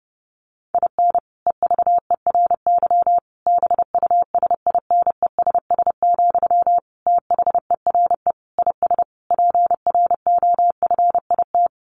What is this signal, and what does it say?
Morse code, heard as a keyed tone, about 30 wpm, 700 Hz. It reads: IN EVERY BUSINESS, THERE IS PROFIT
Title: Sentences from Top 1400 Words 30wpm